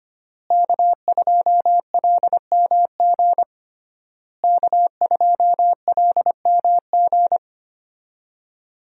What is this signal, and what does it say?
Morse code, heard as a keyed tone, about 25 wpm, 700 Hz. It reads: K2LMG K2LMG
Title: USA Call Signs - 5 Characters - Repeated 3x - 25wpm